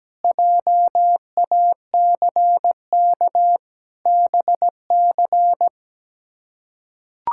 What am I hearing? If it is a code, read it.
Morse code, heard as a keyed tone, about 17 wpm, 700 Hz. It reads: JACK BC
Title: K1USN SST QSO Practice - 17wpm